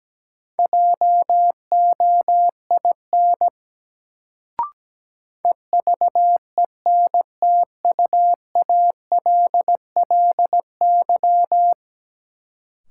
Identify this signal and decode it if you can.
Morse code, heard as a keyed tone, about 17 wpm, 700 Hz. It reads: JOIN EVENTUALLY
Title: Sentences from Top 1000 Words - Review Part 6 - New Word Drill - 17wpm